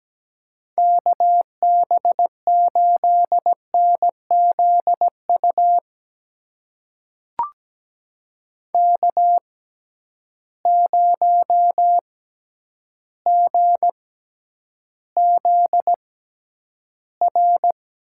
Morse code, heard as a keyed tone, about 17 wpm, spaced as for 9 wpm, 700 Hz. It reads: KB8NZU K0GZR
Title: USA Call Signs - 6x Character Spacing - 17wpm